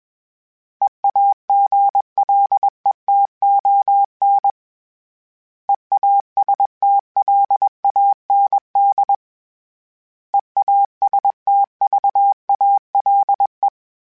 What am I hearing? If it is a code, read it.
Morse code, heard as a keyed tone, about 21 wpm, 800 Hz. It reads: EAGLETON EASTLAND EASTVALE
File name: fromSMs / FromSMs mp3s / Cities@20